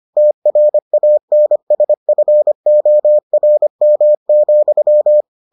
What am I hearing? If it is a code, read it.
Morse code, heard as a keyed tone, about 25 wpm, 600 Hz. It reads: TRANSFORM,